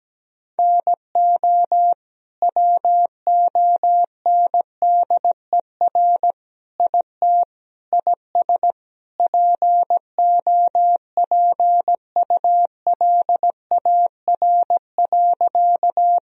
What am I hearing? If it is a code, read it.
Morse code, heard as a keyed tone, about 17 wpm, 700 Hz. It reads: NO WONDER IT IS POPULAR.
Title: Sentences from Top 1000 Words - Review Part 3 - 17wpm